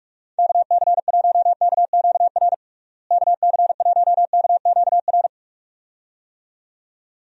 Morse code, heard as a keyed tone, about 42 wpm, 700 Hz. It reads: KC1KQR KC1KQR
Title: USA Call Signs - 6 Characters - Repeated 2x - 45wpm